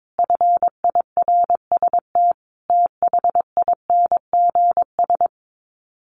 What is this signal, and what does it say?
Morse code, heard as a keyed tone, about 22 wpm, 700 Hz. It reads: FIRST THINGS
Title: Sets of 5 Words Encore 22wpm